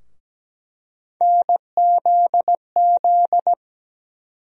Morse code, heard as a keyed tone, about 17 wpm, 700 Hz. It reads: NZZ
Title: USA Call Sign Postfixes - 3 Letters 17wpm